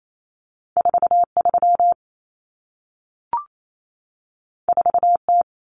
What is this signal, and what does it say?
Morse code, heard as a keyed tone, about 28 wpm, 700 Hz. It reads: 43 4T
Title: Common Cut Numbers - 2-Digit 28wpm